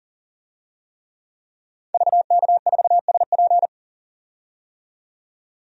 Morse code, heard as a keyed tone, about 38 wpm, 700 Hz. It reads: VK4SP
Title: DXCC-100 Call Signs - 5 Characters - 40wpm